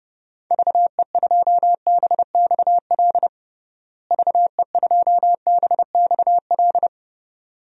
Morse code, heard as a keyed tone, about 30 wpm, 700 Hz. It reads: VE2BXL VE2BXL
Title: DXCC-100 Call Signs - 6 Characters - Repeated 2x - 30wpm